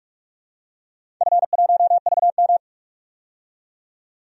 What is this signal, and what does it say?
Morse code, heard as a keyed tone, about 43 wpm, 700 Hz. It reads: F1VM
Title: DXCC-100 Call Signs - 4 Characters - 45wpm